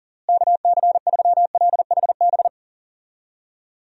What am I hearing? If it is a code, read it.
Morse code, heard as a keyed tone, about 40 wpm, 700 Hz. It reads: KC3LHB
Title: USA Call Signs - 6 Characters - Repeated 3x - 40wpm